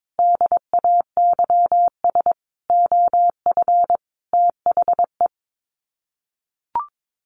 Morse code, heard as a keyed tone, about 22 wpm, 700 Hz. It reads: DAYS OF THE
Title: Sets of 5 Words Encore 22wpm